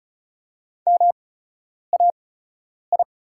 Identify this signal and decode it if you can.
Morse code, heard as a keyed tone, about 34 wpm, spaced as for 10 wpm, 700 Hz. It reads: MAI
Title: 8x Character Spacing - Top 300 Words - 35wpm